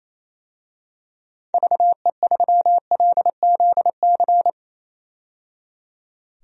Morse code, heard as a keyed tone, about 28 wpm, 700 Hz. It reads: VE3LZC
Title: DXCC-100 Call Signs - 6 Characters - 28wpm